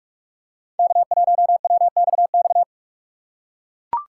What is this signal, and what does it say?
Morse code, heard as a keyed tone, about 43 wpm, 690 Hz. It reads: K1WXX
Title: USA Call Signs - 5 Characters - 2x Character Spacing - 45wpm